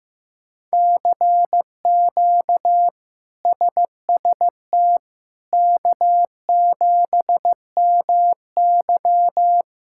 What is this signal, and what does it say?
Morse code, heard as a keyed tone, about 16 wpm, 700 Hz. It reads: CQ SST K7MY
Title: K1USN SST QSO Practice - 15wpm